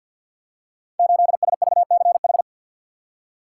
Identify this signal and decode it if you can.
Morse code, heard as a keyed tone, about 48 wpm, 700 Hz. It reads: 8SVCH
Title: Code Groups - 5 Letter:Numbers - 50wpm